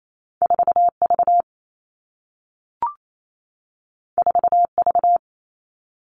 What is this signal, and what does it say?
Morse code, heard as a keyed tone, about 28 wpm, 700 Hz. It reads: 4V 4V
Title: All Cut Numbers - 2-Digit 28wpm